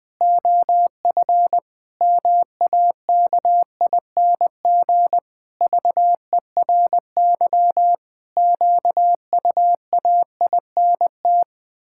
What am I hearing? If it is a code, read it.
Morse code, heard as a keyed tone, about 21 wpm, 700 Hz. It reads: OF MAKING VERY QUAINT
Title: Pangram Challenge v2 - 20wpm